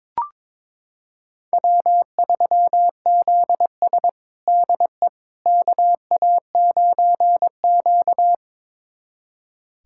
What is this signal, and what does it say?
Morse code, heard as a keyed tone, about 22 wpm, 700 Hz. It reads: W3ZS DE KA9Q
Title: USA Callsign DE Callsign - 4 Characters - 22wpm